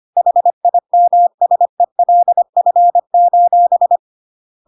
Morse code, heard as a keyed tone, about 25 wpm, 700 Hz. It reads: HIMSELF:
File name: Pinocchio / Pinocchio ch1 25 25 ik1hge